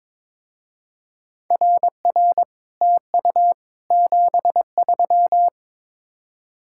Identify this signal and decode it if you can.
Morse code, heard as a keyed tone, about 23 wpm, 700 Hz. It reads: RR TU 73
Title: Speed-Racing Common QSO Phrases - 22wpm